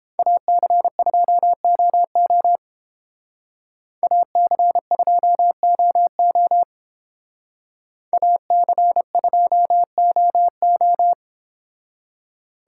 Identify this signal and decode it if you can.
Morse code, heard as a keyed tone, about 28 wpm, 700 Hz. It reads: AC2OO AC2OO AC2OO